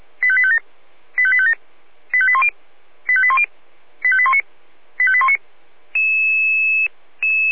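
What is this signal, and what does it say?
Five-tone selective call (ZVEI), heard as quick runs of five tones, tones of about 70 ms. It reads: 85857 85857 85810 85810 85819 85819
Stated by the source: ZVEI five-tone selective call 85850 received on an SDR.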